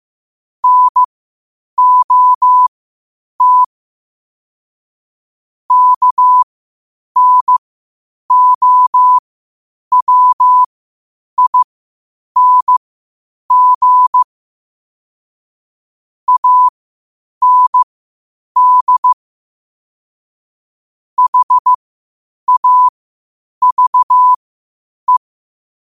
Morse code, heard as a keyed tone, about 15 wpm, spaced as for 8 wpm, 1000 Hz. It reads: NOT KNOWING AND HAVE
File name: qotdfiles / 8wpm